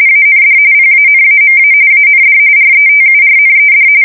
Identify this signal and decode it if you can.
RTTY, about 45.45 baud. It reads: AND PANAMA AND RETURNED
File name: fsk170 / USS-Missouri